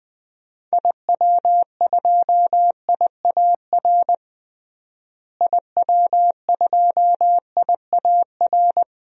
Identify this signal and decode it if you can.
Morse code, heard as a keyed tone, about 20 wpm, 700 Hz. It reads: IW2IAR IW2IAR
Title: DXCC-100 Call Signs - 6 Characters - Repeated 2x - 20wpm